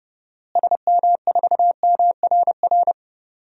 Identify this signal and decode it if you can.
Morse code, heard as a keyed tone, about 30 wpm, 700 Hz. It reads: SM4MRR
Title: DXCC-100 Call Signs - 6 Characters - Repeated 2x - 30wpm